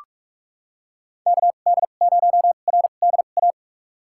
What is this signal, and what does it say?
Morse code, heard as a keyed tone, about 44 wpm, spaced as for 32 wpm, 700 Hz. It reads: KD0RDA
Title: USA Call Signs - 6 Characters - 2x Character Spacing - 45wpm